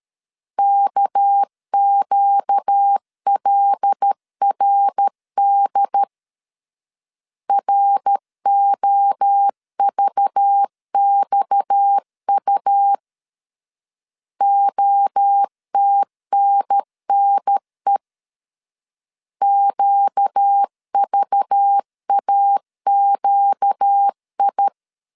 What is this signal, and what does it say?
Morse code, heard as a keyed tone, about 13 wpm, 800 Hz. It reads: KQLRD ROVXU OTNNE QVAQI
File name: CW / DET / 1 6 1500C S B 12wpm 3431